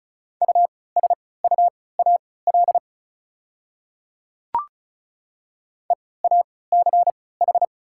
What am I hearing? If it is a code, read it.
Morse code, heard as a keyed tone, about 36 wpm, spaced as for 20 wpm, 690 Hz. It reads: USUAL EACH
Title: 3x Character Spacing - Top 500 Words - 35wpm